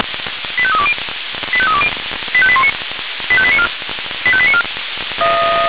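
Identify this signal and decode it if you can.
Five-tone selective call (ZVEI), heard as quick runs of five tones, tones of about 70 ms. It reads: 85320 85320 85819 85894 85894
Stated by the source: ZVEI five-tone call received by SDR.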